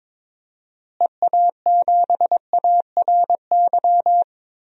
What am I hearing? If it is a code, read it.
Morse code, heard as a keyed tone, about 22 wpm, 700 Hz. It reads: EA7ARY
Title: DXCC-100 Call Signs - 6 Characters - Repeated 3x - 22wpm